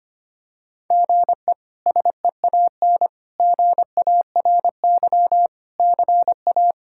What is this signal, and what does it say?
Morse code, heard as a keyed tone, about 25 wpm, 700 Hz. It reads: GE SEAN GARY CA
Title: K1USN SST QSO Practice - 25wpm